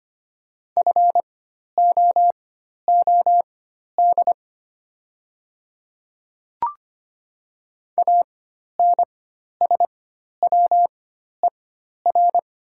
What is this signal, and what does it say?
Morse code, heard as a keyed tone, about 25 wpm, spaced as for 12 wpm, 700 Hz. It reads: FOOD ANSWER
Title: 4x Character Spacing - Top 300 Words - 25wpm